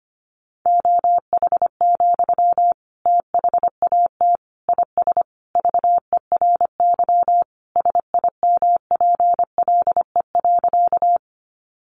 Morse code, heard as a keyed tone, about 25 wpm, 700 Hz. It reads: OH, THAT IS VERY SIMPLE.
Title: Sentences from Top 500 Words 25wpm